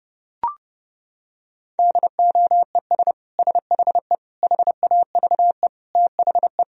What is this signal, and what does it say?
Morse code, heard as a keyed tone, about 30 wpm, 700 Hz. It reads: DOES SHE HAVE THE
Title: Sentences from Top 1000 Words - Review Part 3 - 30wpm